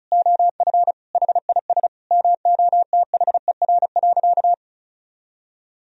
Morse code, heard as a keyed tone, about 36 wpm, 700 Hz. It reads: OF HIS MOTHER.
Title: Sentences from Top 1000 Words - Review Part 4 - 35wpm